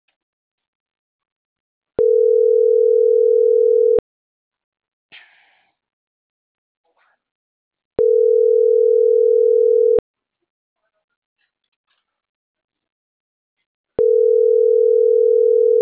Ringback tone (North America), 440 + 480 Hz.
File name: CENTRAL MABE / out antonio.garcia